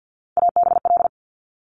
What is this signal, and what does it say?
Morse code, heard as a keyed tone, about 50 wpm, 700 Hz. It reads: ABL
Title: Top Three Letter Combinations in Words 50wpm